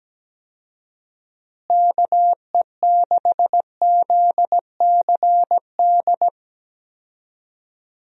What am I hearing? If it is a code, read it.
Morse code, heard as a keyed tone, about 17 wpm, 700 Hz. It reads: KE6ZCD
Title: USA Call Signs - 6 Characters - Repeated 2x - 17wpm